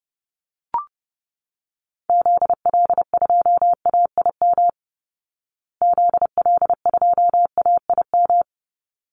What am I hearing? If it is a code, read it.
Morse code, heard as a keyed tone, about 30 wpm, 700 Hz. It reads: ZL2AIM ZL2AIM
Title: Call Signs Repeated 3x - 30wpm